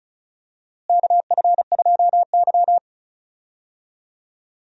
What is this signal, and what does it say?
Morse code, heard as a keyed tone, about 34 wpm, 700 Hz. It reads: KF2Y
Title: USA Call Signs - 4 Characters - Repeated 2x - 35wpm